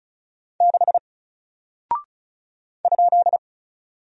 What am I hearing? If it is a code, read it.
Morse code, heard as a keyed tone, about 36 wpm, 700 Hz. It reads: 6 ?
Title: Single Character (Letter:Number:Punctuation:Prosign) 35wpm